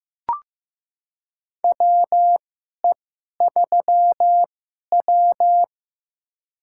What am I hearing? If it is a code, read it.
Morse code, heard as a keyed tone, about 15 wpm, spaced as for 11 wpm, 700 Hz. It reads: WE3W
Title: USA Call Signs - 4 Characters - 2x Character Spacing - 15wpm